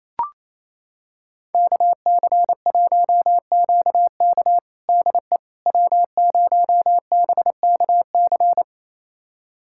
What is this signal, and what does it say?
Morse code, heard as a keyed tone, about 28 wpm, 700 Hz. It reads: KC1QK DE W0BKC
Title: USA Callsign DE Callsign - 5 Characters - 28wpm